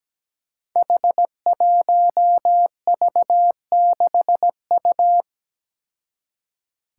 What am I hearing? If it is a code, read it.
Morse code, heard as a keyed tone, about 17 wpm, 700 Hz. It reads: H1V6U
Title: Five Challenging Letter-Number Combinations 17wpm